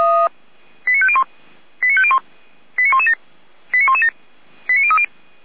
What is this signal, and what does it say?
Five-tone selective call (ZVEI), heard as quick runs of five tones, tones of about 70 ms. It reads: 89501 89501 89117 89117 89030